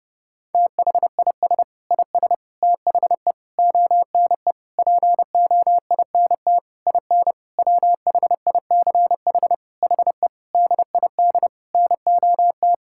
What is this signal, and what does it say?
Morse code, heard as a keyed tone, about 30 wpm, 700 Hz. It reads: THIS IS THE ONE POINT IN WHICH HE DID NOT
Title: Sentences from Top 1000 Words - Review Part 5 - 30wpm